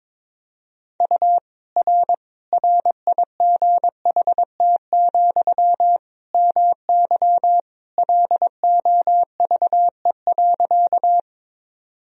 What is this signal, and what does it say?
Morse code, heard as a keyed tone, about 22 wpm, 700 Hz. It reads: U R RIGHT, MY LOVE.
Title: Sentences from Top 400 Words and Abbreviations 22wpm